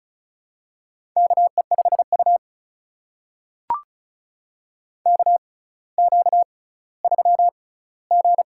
Morse code, heard as a keyed tone, about 36 wpm, spaced as for 18 wpm, 700 Hz. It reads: KE5U KQ3G
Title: USA Call Signs - 4 Characters - 6x Character Spacing - 35wpm